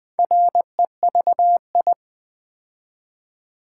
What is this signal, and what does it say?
Morse code, heard as a keyed tone, about 20 wpm, 700 Hz. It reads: REVI
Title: Build A Word - 2 Letters At A Time - 6-Letter Words - 20wpm